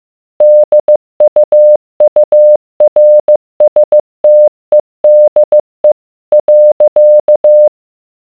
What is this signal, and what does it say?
Morse code, heard as a keyed tone, about 15 wpm, 600 Hz. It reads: DUURSTEDE .